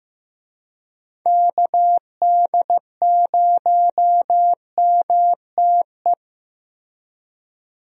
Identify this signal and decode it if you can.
Morse code, heard as a keyed tone, about 15 wpm, 700 Hz. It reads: KD0MTE
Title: USA Call Signs - 6 Characters - Repeated 2x - 15wpm